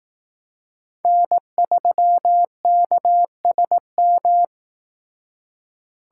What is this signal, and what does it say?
Morse code, heard as a keyed tone, about 18 wpm, 700 Hz. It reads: N3KSM
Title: Speed Racing - USA 1x3 Call Signs 15wpm